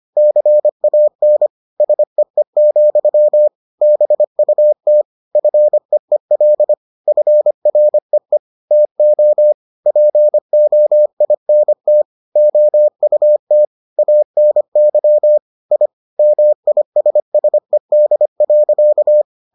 Morse code, heard as a keyed tone, about 25 wpm, 600 Hz. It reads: CAN SEE, BUT FEEL FREE TO POINT OUT ANY I MISSED.